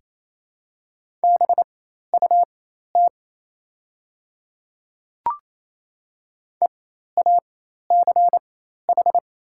Morse code, heard as a keyed tone, about 28 wpm, spaced as for 13 wpm, 700 Hz. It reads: BUT EACH
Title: 4x Character Spacing - Top 100 Words - 28wpm